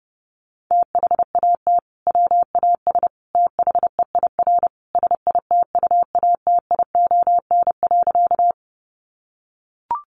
Morse code, heard as a keyed tone, about 30 wpm, 700 Hz. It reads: THAT WAS THEIR SITUATION.
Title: Sentences from Top 1200 Words 30wpm